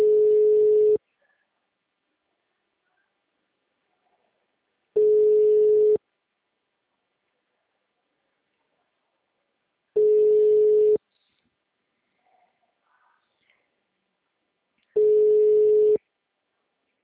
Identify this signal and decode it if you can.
Ringback tone (Europe, 425 Hz), with speech.